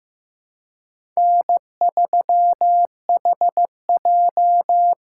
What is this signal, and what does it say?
Morse code, heard as a keyed tone, about 15 wpm, 700 Hz. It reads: N3HJ
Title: USA Call Signs - 4 Characters - 3x Character Spacing - 15wpm